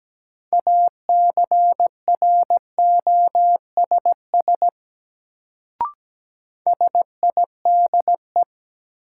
Morse code, heard as a keyed tone, about 17 wpm, 700 Hz. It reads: ACROSS SIDE
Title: Sets of 2 Words - New Word Drill - 17wpm